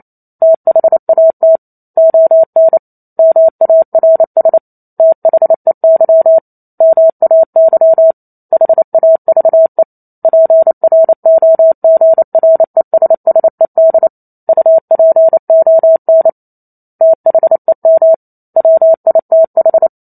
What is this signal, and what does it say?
Morse code, heard as a keyed tone, about 28 wpm, 650 Hz. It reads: THAT ON MARS THEY MAY HAVE PROGRESSED UPON THEM WITH
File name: Chapter 2 - 28 WPM